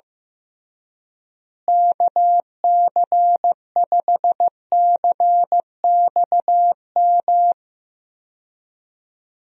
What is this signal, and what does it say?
Morse code, heard as a keyed tone, about 15 wpm, 700 Hz. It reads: KC5CXM
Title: USA Call Signs - 6 Characters - Repeated 3x - 15wpm